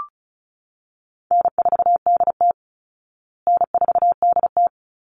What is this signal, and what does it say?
Morse code, heard as a keyed tone, about 34 wpm, 700 Hz. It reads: N4DT N4DT
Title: Call Signs Repeated 2x - 35wpm